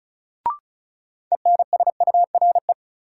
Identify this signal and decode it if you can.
Morse code, heard as a keyed tone, about 36 wpm, 700 Hz. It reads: ENSURE
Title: Sentences from Top 1100 Words - New Word Drill Alt - 35wpm